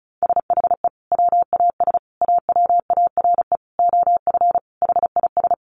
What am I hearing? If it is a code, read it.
Morse code, heard as a keyed tone, about 34 wpm, 700 Hz. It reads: SHE WAS AWARE OF HIS
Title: Sentences from Top 1200 Words 35wpm